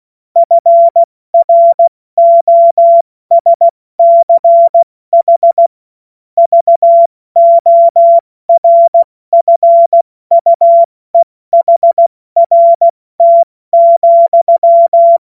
Morse code, heard as a keyed tone, about 16 wpm, 680 Hz. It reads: FROSCH VORFUEHRT,
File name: Morsekurs / Witze / Witz 05. April 2022 70 F80